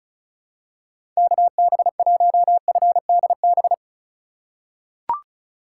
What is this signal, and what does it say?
Morse code, heard as a keyed tone, about 36 wpm, 700 Hz. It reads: KB1FDB
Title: USA Call Signs - 2x Character Spacing - 35wpm